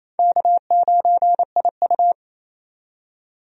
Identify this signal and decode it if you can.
Morse code, heard as a keyed tone, about 27 wpm, 700 Hz. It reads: K9IU
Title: USA Call Signs - 4 Characters - 28wpm